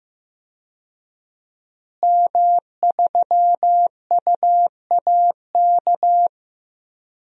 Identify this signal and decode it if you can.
Morse code, heard as a keyed tone, about 15 wpm, 700 Hz. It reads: M3UAK
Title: DXCC-100 Call Signs - 5 Characters - Repeated 2x - 15wpm